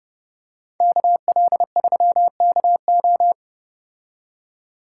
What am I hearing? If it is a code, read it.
Morse code, heard as a keyed tone, about 29 wpm, 700 Hz. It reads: KL3KO